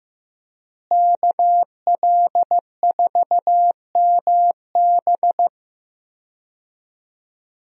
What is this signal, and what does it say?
Morse code, heard as a keyed tone, about 15 wpm, 700 Hz. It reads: KL4MB
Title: USA Call Signs - 5 Characters - Repeated 2x - 15wpm